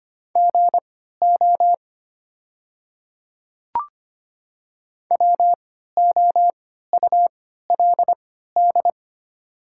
Morse code, heard as a keyed tone, about 25 wpm, spaced as for 14 wpm, 700 Hz. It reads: GO WOULD
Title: 3x Character Spacing - Top 100 Words - 25wpm